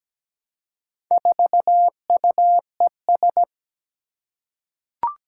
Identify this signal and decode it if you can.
Morse code, heard as a keyed tone, about 17 wpm, 700 Hz. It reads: 4UES
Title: Four Challenging Letter-Number Combinations 17wpm